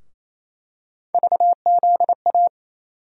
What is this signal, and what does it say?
Morse code, heard as a keyed tone, about 28 wpm, 700 Hz. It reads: VZA
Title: USA Call Sign Postfixes - 3 Letters 28wpm